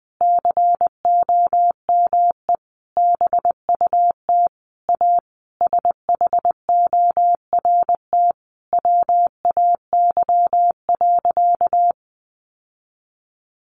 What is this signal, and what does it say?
Morse code, heard as a keyed tone, about 20 wpm, 700 Hz. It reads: COME BUT A SHORT WAY.
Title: Sentences from Top 1400 Words 20wpm